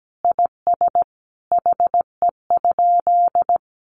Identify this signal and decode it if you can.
Morse code, heard as a keyed tone, about 17 wpm, 700 Hz. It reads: IS HE?
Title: Sentences from Top 1300 Words 17wpm